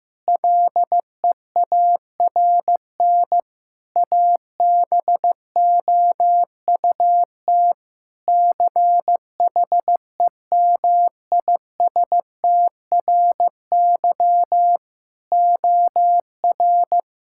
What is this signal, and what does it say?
Morse code, heard as a keyed tone, about 15 wpm, 700 Hz. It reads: LEARN ABOUT CHEMISTRY OR
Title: Sentences from Top 1000 Words - Review Part 6 - 15wpm